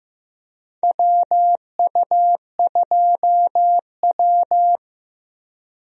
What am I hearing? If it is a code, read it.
Morse code, heard as a keyed tone, about 15 wpm, 700 Hz. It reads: WU2W
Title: USA Call Signs - 4 Characters - Repeated 3x - 15wpm